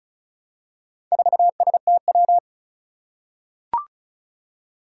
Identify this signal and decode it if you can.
Morse code, heard as a keyed tone, about 36 wpm, 700 Hz. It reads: 4STW
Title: Four Challenging Letter-Number Combinations 35wpm